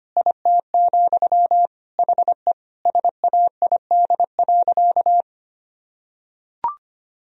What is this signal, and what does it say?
Morse code, heard as a keyed tone, about 25 wpm, 700 Hz. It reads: IT, HE SAID.
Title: Sentences from Top 1000 Words - Review Part 4 - 25wpm